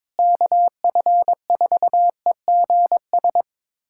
Morse code, heard as a keyed tone, about 22 wpm, 700 Hz. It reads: KF4EGS
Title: USA Callsign DE Callsign - 6 Characters - 22wpm